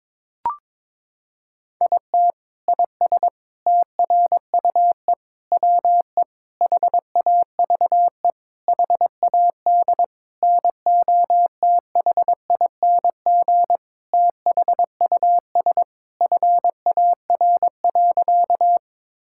Morse code, heard as a keyed tone, about 22 wpm, 700 Hz. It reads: IT IS TRUE WE HAVE HAD NOTHING THUS FAR.
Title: Sentences from Top 1000 Words - Review Part 5 - 22wpm